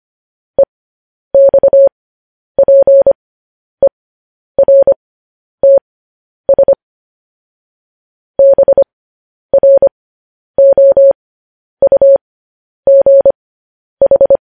Morse code, heard as a keyed tone, about 25 wpm, spaced as for 10 wpm, 550 Hz. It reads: EXPERTS BROUGH
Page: news-headlines-in-morse-code-at-10-wpm